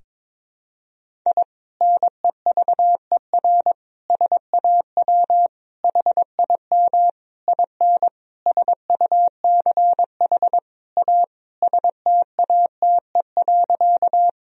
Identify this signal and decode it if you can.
Morse code, heard as a keyed tone, about 21 wpm, 700 Hz. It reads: I NEVER SAW HIM IN SUCH A STATE.
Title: Sentences from Top 1000 Words - Review Part 2 - 22wpm